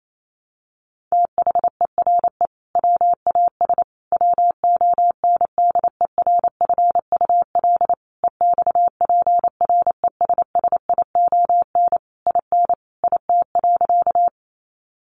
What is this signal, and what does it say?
Morse code, heard as a keyed tone, about 28 wpm, 700 Hz. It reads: THERE WAS WONDERFUL EXPRESSION IN IT.
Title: Sentences from Top 1200 Words 28wpm